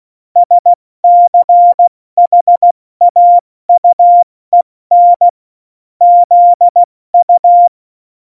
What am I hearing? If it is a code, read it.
Morse code, heard as a keyed tone, about 16 wpm, 700 Hz. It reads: SCHAUEN ZU
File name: Morsekurs / Witze / Witz 29. Mai 2018, 70 F80